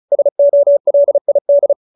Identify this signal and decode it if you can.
Morse code, heard as a keyed tone, about 34 wpm, 590 Hz. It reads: SOLID